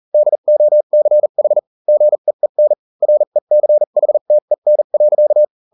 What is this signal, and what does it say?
Morse code, heard as a keyed tone, about 40 wpm, spaced as for 35 wpm, 600 Hz. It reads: DOCH GEEN RECHTEN.